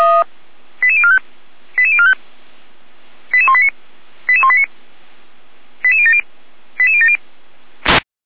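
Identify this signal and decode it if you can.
Five-tone selective call (ZVEI), heard as quick runs of five tones, tones of about 80 ms. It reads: 88045 88045 88189 88189 88070 88070
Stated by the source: ZVEI five-tone selective call received by SDR, from the feuerwehr-orlamuende.org archive.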